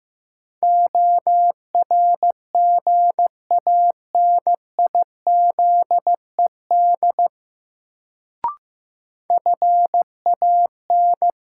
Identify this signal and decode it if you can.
Morse code, heard as a keyed tone, about 15 wpm, 700 Hz. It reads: ORGANIZED FAN
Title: Sentences from Top 1300 Words - New Word Drill Alt - 15wpm